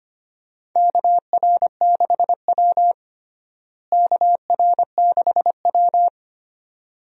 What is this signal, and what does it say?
Morse code, heard as a keyed tone, about 25 wpm, 700 Hz. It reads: KR6W KR6W
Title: USA Call Signs - 4 Characters - Repeated 2x - 25wpm